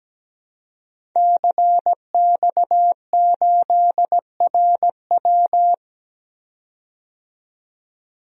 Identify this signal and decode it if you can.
Morse code, heard as a keyed tone, about 17 wpm, 700 Hz. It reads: CX8RW
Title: DXCC-100 Call Signs - 5 Characters - 17wpm